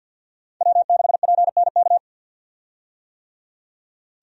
Morse code, heard as a keyed tone, about 51 wpm, 700 Hz. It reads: W6PNK
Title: USA Call Signs - 5 Characters - 50wpm